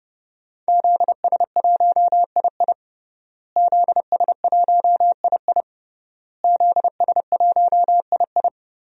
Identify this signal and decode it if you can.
Morse code, heard as a keyed tone, about 30 wpm, 700 Hz. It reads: ZS1II ZS1II ZS1II
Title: DXCC-100 Call Signs - 5 Characters - Repeated 3x - 30wpm